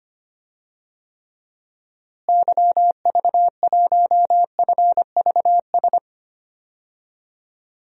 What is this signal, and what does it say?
Morse code, heard as a keyed tone, about 25 wpm, 700 Hz. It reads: YV1FVS
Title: DXCC-100 Call Signs - 6 Characters - Repeated 2x - 25wpm